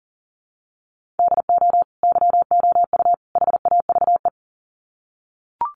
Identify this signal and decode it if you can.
Morse code, heard as a keyed tone, about 40 wpm, 700 Hz. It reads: DO YOU HAVE
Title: Sets of 3 Words 40wpm